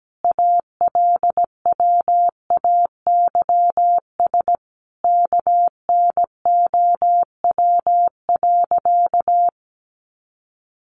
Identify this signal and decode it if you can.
Morse code, heard as a keyed tone, about 17 wpm, 700 Hz. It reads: ALWAYS KNOW.
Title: Sentences from Top 300 Words 17wpm